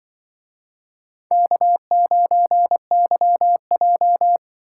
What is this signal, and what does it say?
Morse code, heard as a keyed tone, about 23 wpm, 700 Hz. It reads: K9YJ
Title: Speed Racing - USA 1x2 Call Signs 20wpm